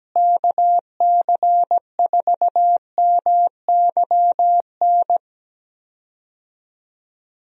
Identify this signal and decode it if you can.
Morse code, heard as a keyed tone, about 17 wpm, 700 Hz. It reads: KC4MYN
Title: USA Call Signs - 6 Characters - Repeated 2x - 17wpm